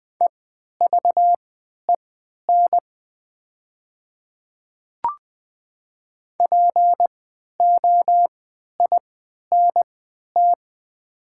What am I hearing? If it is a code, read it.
Morse code, heard as a keyed tone, about 20 wpm, spaced as for 11 wpm, 690 Hz. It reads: EVEN POINT
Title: 3x Character Spacing - Top 300 Words - 20wpm